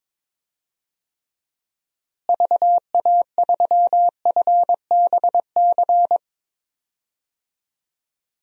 Morse code, heard as a keyed tone, about 21 wpm, 700 Hz. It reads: VA3FBC
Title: DXCC-100 Call Signs - 6 Characters - Repeated 2x - 22wpm